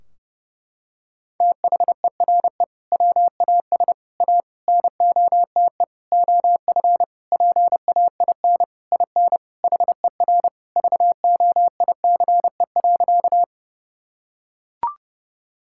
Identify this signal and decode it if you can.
Morse code, heard as a keyed tone, about 30 wpm, 700 Hz. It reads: THERE WAS A NOTE OF PAIN IN HER VOICE.
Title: Sentences from Top 1600 Words 30wpm